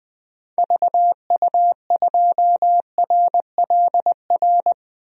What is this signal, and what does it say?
Morse code, heard as a keyed tone, about 20 wpm, 700 Hz. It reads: VU2RLR
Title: DXCC-100 Call Signs - 6 Characters - Repeated 3x - 20wpm